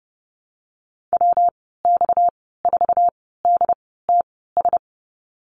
Morse code, heard as a keyed tone, about 30 wpm, spaced as for 17 wpm, 700 Hz. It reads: WX4DTS
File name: Lesson 37g - Letter x - Call Signs 30wpm